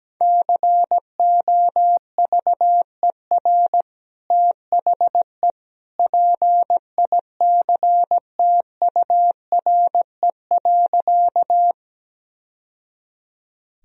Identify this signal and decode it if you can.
Morse code, heard as a keyed tone, about 17 wpm, 700 Hz. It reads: COVER THE PICTURE.
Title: Sentences from Top 1000 Words - Review Part 1 - 17wpm